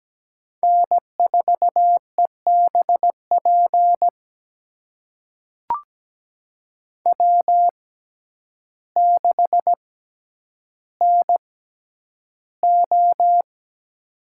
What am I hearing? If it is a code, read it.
Morse code, heard as a keyed tone, about 17 wpm, spaced as for 9 wpm, 700 Hz. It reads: N4EBP W6NO
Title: USA Call Signs - 6x Character Spacing - 17wpm